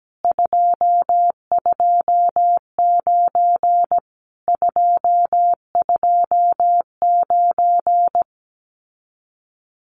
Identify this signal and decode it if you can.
Morse code, heard as a keyed tone, about 17 wpm, 700 Hz. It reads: 229 229
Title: Common QSO Phrases Encore 17wpm